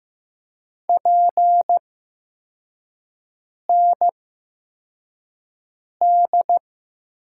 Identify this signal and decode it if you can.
Morse code, heard as a keyed tone, about 15 wpm, spaced as for 4 wpm, 700 Hz. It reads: PND
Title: USA Call Signs - 6 Characters - 8x Character Spacing - 15wpm